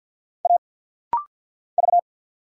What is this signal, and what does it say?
Morse code, heard as a keyed tone, about 49 wpm, 700 Hz. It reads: A V
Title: Single Letter-Number - Rapid-Fire 50wpm